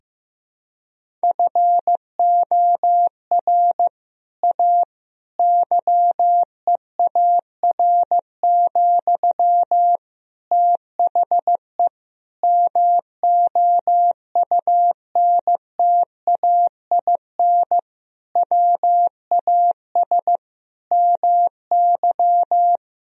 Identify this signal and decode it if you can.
Morse code, heard as a keyed tone, about 15 wpm, 700 Hz. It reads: FOR A YEAR, THE MOUNTAIN WAS MY
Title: Sentences from Top 1000 Words - Review Part 1 - 15wpm